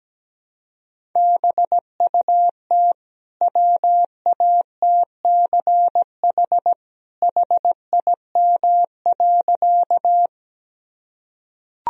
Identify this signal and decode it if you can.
Morse code, heard as a keyed tone, about 17 wpm, 700 Hz. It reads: BUT WATCH HIM.
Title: Sentences from Top 1000 Words - Review Part 1 - 17wpm